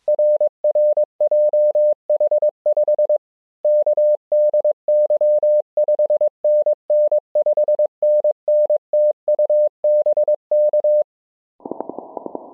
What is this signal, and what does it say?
Morse code, heard as a keyed tone, about 22 wpm, 600 Hz. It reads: RR JH5KDY 5NN 5NN TU BK